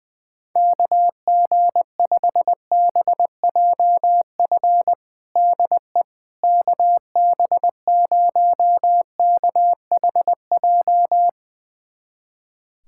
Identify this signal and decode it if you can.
Morse code, heard as a keyed tone, about 20 wpm, 700 Hz. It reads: KG5BJF DE KB0KHJ
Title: USA Callsign DE Callsign - 6 Characters - 20wpm